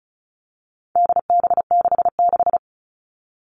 Morse code, heard as a keyed tone, about 34 wpm, 700 Hz. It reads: DB66
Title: All Cut Numbers - 4-Digit 35wpm